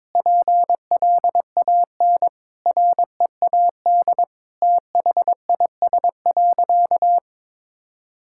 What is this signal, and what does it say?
Morse code, heard as a keyed tone, about 22 wpm, 700 Hz. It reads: PLAN READ THIS.
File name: practice / 22wpm / Sentences with 8-Word Phrases - 22wpm